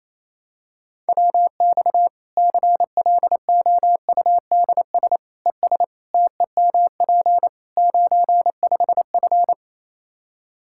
Morse code, heard as a keyed tone, about 28 wpm, 700 Hz. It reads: WX CLOUDS ES TEMP 95F
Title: Practice QSO - Exchange 2 - Info - 28wpm